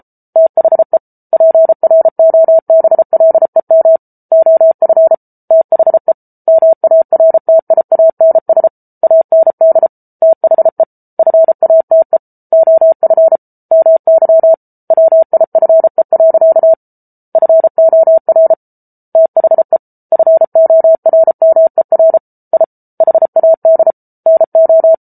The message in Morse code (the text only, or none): THE PROBLEM OF THE MARTIANS AND THE FATE OF MY WIFE. FOR THE FORMER I HAD NO